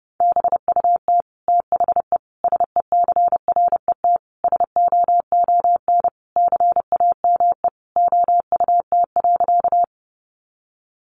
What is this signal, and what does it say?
Morse code, heard as a keyed tone, about 30 wpm, 700 Hz. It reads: BUT THE SECRET SOON CAME OUT.
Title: Sentences from Top 1200 Words 30wpm